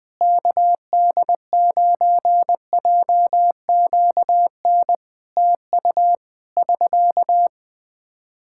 Morse code, heard as a keyed tone, about 20 wpm, 700 Hz. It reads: KD9JQN TU <SK>
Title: Practice QSO - Exchange 4 - Ending - 20wpm